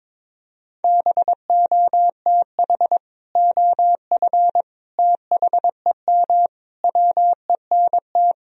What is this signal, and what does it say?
Morse code, heard as a keyed tone, about 22 wpm, 700 Hz. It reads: BOTH OF THEM WENT
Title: Sentences from Top 1000 Words - Review Part 1 - 22wpm